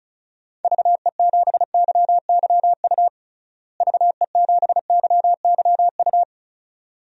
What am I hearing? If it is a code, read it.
Morse code, heard as a keyed tone, about 36 wpm, 700 Hz. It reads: VE7YYU VE7YYU
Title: DXCC-100 Call Signs - 6 Characters - Repeated 2x - 35wpm